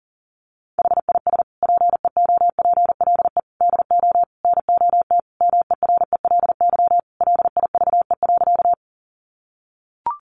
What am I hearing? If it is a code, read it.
Morse code, heard as a keyed tone, about 40 wpm, 700 Hz. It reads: HIS PEOPLE DO NOT MERELY LIVE.
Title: Sentences from Top 1400 Words 40wpm